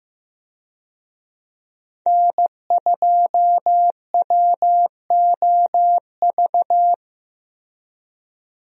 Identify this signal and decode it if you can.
Morse code, heard as a keyed tone, about 15 wpm, 700 Hz. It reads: N2WOV
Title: USA Call Signs - 5 Characters - Repeated 2x - 15wpm